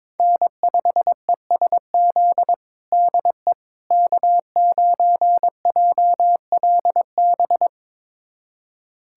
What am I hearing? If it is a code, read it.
Morse code, heard as a keyed tone, about 22 wpm, 700 Hz. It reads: N5ESZ DE K9JLB
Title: USA Callsign DE Callsign - 5 Characters - 22wpm